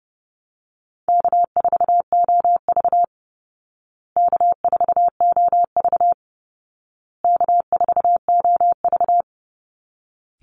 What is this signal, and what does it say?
Morse code, heard as a keyed tone, about 30 wpm, 700 Hz. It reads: K4OV K4OV K4OV
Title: Call Signs Repeated 2x - 30wpm